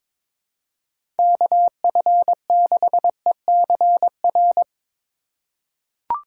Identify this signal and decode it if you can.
Morse code, heard as a keyed tone, about 22 wpm, 700 Hz. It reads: KF6ECR
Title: USA Call Signs - 6 Characters - 3x Character Spacing - 22wpm